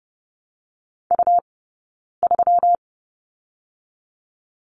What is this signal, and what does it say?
Morse code, heard as a keyed tone, about 30 wpm, spaced as for 9 wpm, 700 Hz. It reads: U3
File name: Lesson 20d - Number 3 - Two Characters 30wpm